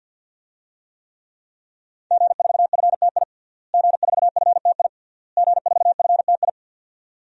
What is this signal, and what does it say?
Morse code, heard as a keyed tone, about 51 wpm, 700 Hz. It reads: G4FTI G4FTI G4FTI
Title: DXCC-100 Call Signs - 5 Characters - Repeated 3x - 50wpm